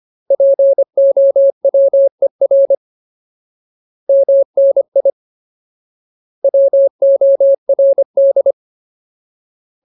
Morse code, heard as a keyed tone, about 25 wpm, 550 Hz. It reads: POWER MNI WORD
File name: Ultimate Word List / 25WPM Ultimate